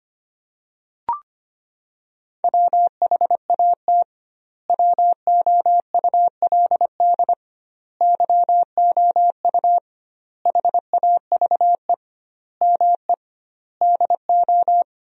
Morse code, heard as a keyed tone, about 25 wpm, 700 Hz. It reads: WHAT WOULD YOU HAVE ME DO
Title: 2x Word-Spacing - Sentences from Top 300 Words - 25wpm